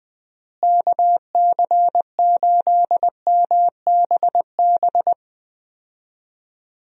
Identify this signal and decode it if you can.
Morse code, heard as a keyed tone, about 20 wpm, 700 Hz. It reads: KC8MBB
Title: USA Call Signs - 6 Characters - Repeated 2x - 20wpm